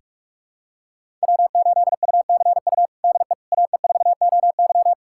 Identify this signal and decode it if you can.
Morse code, heard as a keyed tone, about 41 wpm, 700 Hz. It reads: W8UCU DE AE4OY
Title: USA Callsign DE Callsign 45wpm